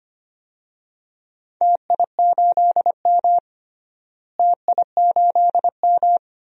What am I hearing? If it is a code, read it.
Morse code, heard as a keyed tone, about 25 wpm, 700 Hz. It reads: TI8M TI8M
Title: DXCC-100 Call Signs - 4 Characters - Repeated 2x - 25wpm